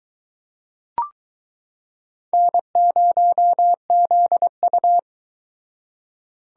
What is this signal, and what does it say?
Morse code, heard as a keyed tone, about 22 wpm, 700 Hz. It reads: N0ZU